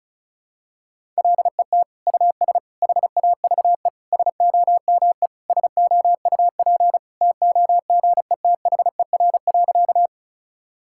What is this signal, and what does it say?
Morse code, heard as a keyed tone, about 36 wpm, 700 Hz. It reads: LET US HAVE SOME SOUP TOGETHER.
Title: Sentences from Top 1000 Words - Review Part 6 - 35wpm